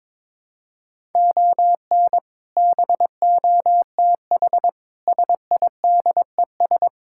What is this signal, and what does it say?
Morse code, heard as a keyed tone, about 22 wpm, 700 Hz. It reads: ON BOTH SIDES
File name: practice / 22wpm / Build A Phrase - 5-Word Phrases - 22wpm